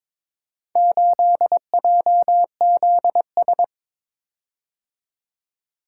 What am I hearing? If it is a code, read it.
Morse code, heard as a keyed tone, about 23 wpm, 700 Hz. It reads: 8JZS
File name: practice / 22wpm / Four Letter-Number Combinations - 22wpm